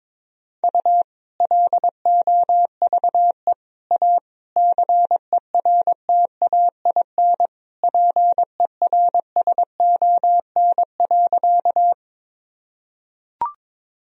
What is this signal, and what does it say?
Morse code, heard as a keyed tone, about 22 wpm, 700 Hz. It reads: U LOVE A CERTAIN PERSON.
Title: Sentences from Top 500 Words and Abbreviations 22wpm